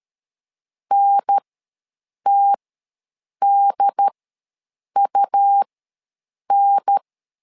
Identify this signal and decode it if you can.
Morse code, heard as a keyed tone, about 13 wpm, spaced as for 7 wpm, 800 Hz. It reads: NTDUN
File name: LEZIONI / 2 UHDN / 2 UHDN ws 12wpm 1708